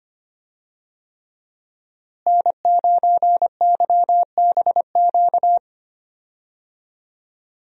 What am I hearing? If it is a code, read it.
Morse code, heard as a keyed tone, about 25 wpm, 700 Hz. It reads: N9YBQ
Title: USA Call Signs - 5 Characters - Repeated 3x - 25wpm